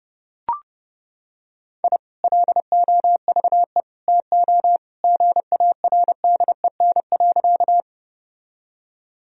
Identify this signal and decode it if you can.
Morse code, heard as a keyed tone, about 32 wpm, 700 Hz. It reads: I LOVE TO GARDEN.
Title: Sentences from Top 1000 Words - Review Part 4 - 30wpm